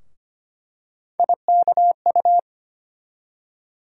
Morse code, heard as a keyed tone, about 25 wpm, 700 Hz. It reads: IKU
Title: USA Call Sign Postfixes - 3 Letters 25wpm